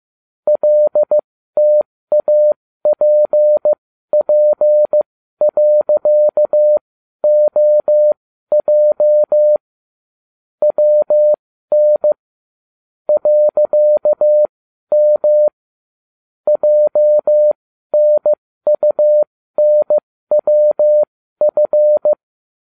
Morse code, heard as a keyed tone, about 15 wpm, spaced as for 12 wpm, 600 Hz. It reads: LTAPP.OJ WN .M JNUNWF